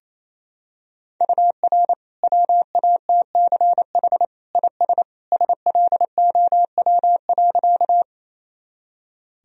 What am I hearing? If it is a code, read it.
Morse code, heard as a keyed tone, about 28 wpm, 700 Hz. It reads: UR WATCH IS SLOW.
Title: Sentences from Top 400 Words and Abbreviations 28wpm